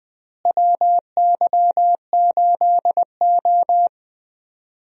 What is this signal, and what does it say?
Morse code, heard as a keyed tone, about 20 wpm, 700 Hz. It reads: WY8O
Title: USA Call Signs - 4 Characters - Repeated 3x - 20wpm